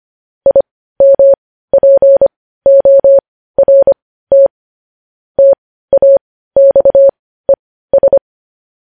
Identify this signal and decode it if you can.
Morse code, heard as a keyed tone, about 25 wpm, spaced as for 15 wpm, 550 Hz. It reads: IMPORT TAXES
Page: news headlines in morse code at 15 wpm